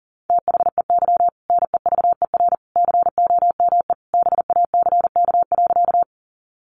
Morse code, heard as a keyed tone, about 40 wpm, 700 Hz. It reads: THEY NEVER COME BACK.
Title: Sentences from Top 300 Words 40wpm